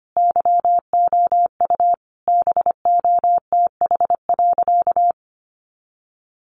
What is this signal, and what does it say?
Morse code, heard as a keyed tone, about 25 wpm, 700 Hz. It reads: YOU BOTH.
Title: Sentences from Top 300 Words 25wpm